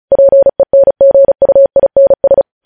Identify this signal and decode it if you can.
Morse code, heard as a keyed tone, about 36 wpm, 550 Hz. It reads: PENGUINS
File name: podcast / News headlines-1x-35x35